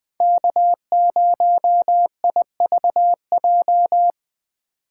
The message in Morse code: K0IVJ

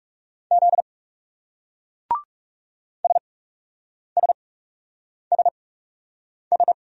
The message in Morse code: Z SSSS

